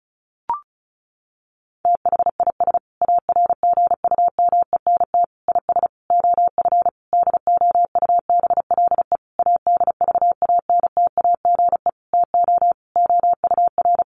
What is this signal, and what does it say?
Morse code, heard as a keyed tone, about 34 wpm, 700 Hz. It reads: THIS ARGUMENT IS OF DOUBLE ADVANTAGE TO OUR
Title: Sentences from Top 1400 Words 35wpm